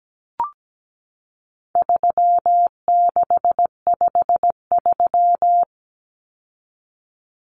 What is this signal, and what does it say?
Morse code, heard as a keyed tone, about 17 wpm, 700 Hz. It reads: 3653
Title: Common Cut Numbers - 4-Digit 17wpm